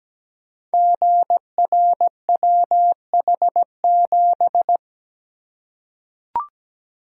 Morse code, heard as a keyed tone, about 17 wpm, 700 Hz. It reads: GRWH7
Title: Code Groups - 5 Letter:Numbers - 17wpm